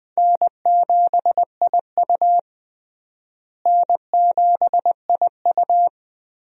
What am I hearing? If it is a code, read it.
Morse code, heard as a keyed tone, about 20 wpm, 700 Hz. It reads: N7IU N7IU
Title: USA Call Signs - 4 Characters - Repeated 3x - 20wpm